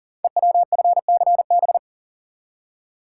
Morse code, heard as a keyed tone, about 42 wpm, 700 Hz. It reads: EWFCB
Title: Code Groups - 5 Letters - 40wpm